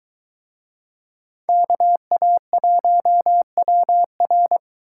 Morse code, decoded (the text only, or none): KA1WR